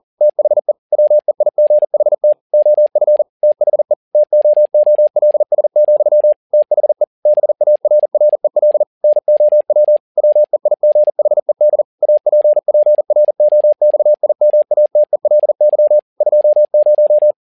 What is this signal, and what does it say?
Morse code, heard as a keyed tone, about 40 wpm, 600 Hz. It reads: THE WEIGHT OF THE TOOLS, THE BARREL NOW WEIGHED APPROXIMATELY 20
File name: learning center / tower trouble / tower touble 40wpm